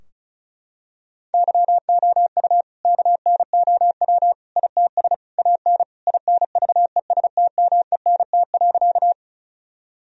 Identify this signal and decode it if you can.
Morse code, heard as a keyed tone, about 34 wpm, 700 Hz. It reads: YOU KNOW ITS AN INVESTMENT.
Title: Sentences from Top 1000 Words - Review Part 4 - 35wpm